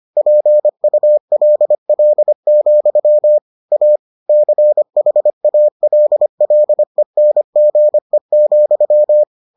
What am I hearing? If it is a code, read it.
Morse code, heard as a keyed tone, about 25 wpm, 600 Hz. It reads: PULL, A CHALLENGE,